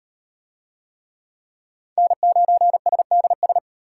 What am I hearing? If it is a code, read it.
Morse code, heard as a keyed tone, about 33 wpm, 700 Hz. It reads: N9SDS